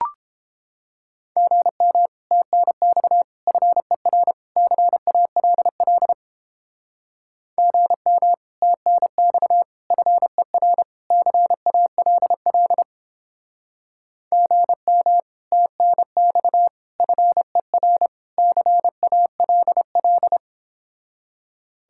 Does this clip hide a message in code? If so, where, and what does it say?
Yes: Morse code from 1.4 to 20.4 s, heard as a keyed tone, about 29 wpm, 700 Hz. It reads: GM TNX FER CALL GM TNX FER CALL GM TNX FER CALL